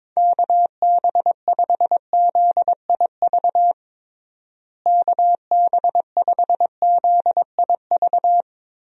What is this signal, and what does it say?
Morse code, heard as a keyed tone, about 21 wpm, 700 Hz. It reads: KB5ZIV KB5ZIV
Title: USA Call Signs - 6 Characters - Repeated 3x - 22wpm